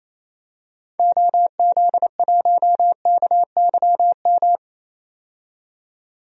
Morse code, heard as a keyed tone, about 28 wpm, 700 Hz. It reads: OZ1KYM
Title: DXCC-100 Call Signs - 6 Characters - 28wpm